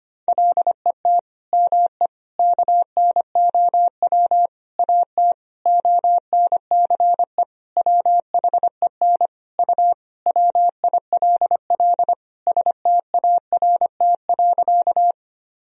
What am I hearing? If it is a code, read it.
Morse code, heard as a keyed tone, about 25 wpm, 700 Hz. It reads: LET ME KNOW AT ONCE WHEN U WILL START.
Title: Sentences from Top 300 Words and Abbreviations 25wpm